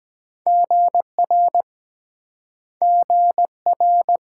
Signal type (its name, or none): Morse code